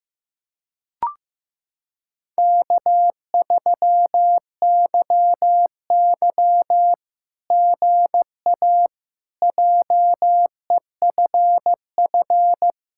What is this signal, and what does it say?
Morse code, heard as a keyed tone, about 15 wpm, 700 Hz. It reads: K3YY GA JEFF
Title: SOTA QSO Practice 15wpm